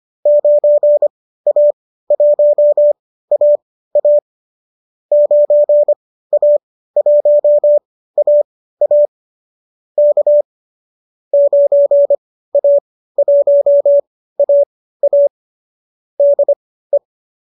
Morse code, heard as a keyed tone, about 25 wpm, spaced as for 15 wpm, 590 Hz. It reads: 9A1AA 9A1AA K 9A1AA DE
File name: qso219 15